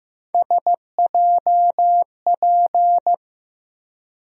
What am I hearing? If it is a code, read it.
Morse code, heard as a keyed tone, about 15 wpm, 700 Hz. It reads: SJP
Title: USA Call Sign Postfixes - 3 Letters 15wpm